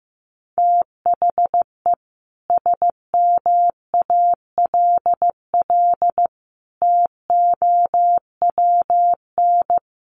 Morse code, heard as a keyed tone, about 15 wpm, 700 Hz. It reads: THE SMALL TOWN
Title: Sentences from Top 1300 Words 15wpm